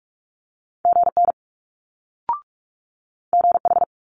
Morse code, heard as a keyed tone, about 46 wpm, 700 Hz. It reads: GN GH
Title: Multi-letter Phonemes Quick Fire 45wpm